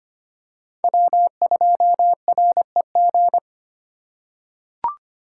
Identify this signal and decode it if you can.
Morse code, heard as a keyed tone, about 25 wpm, 700 Hz. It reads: W2REG
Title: USA Call Signs - 5 Characters - 2x Character Spacing - 25wpm